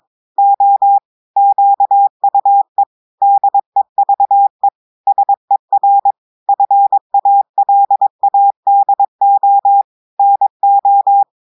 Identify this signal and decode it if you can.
Morse code, heard as a keyed tone, about 22 wpm, 800 Hz. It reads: O QUE DEVE SER FALADO NO
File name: audios / portugais / portugais texte en clair 2 vitesse 22 mots minutes